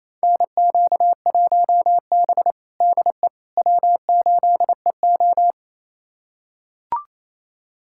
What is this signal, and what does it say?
Morse code, heard as a keyed tone, about 28 wpm, 700 Hz. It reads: NQ1B DE W8EO
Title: USA Callsign DE Callsign - 4 Characters - 28wpm